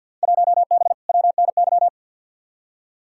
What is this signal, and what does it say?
Morse code, heard as a keyed tone, about 47 wpm, 700 Hz. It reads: 1B WNY
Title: Field Day Exchanges Encore - 50wpm